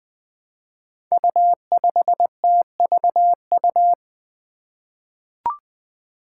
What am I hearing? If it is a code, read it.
Morse code, heard as a keyed tone, about 20 wpm, 700 Hz. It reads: U5TVU
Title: Five Challenging Letter-Number Combinations 20wpm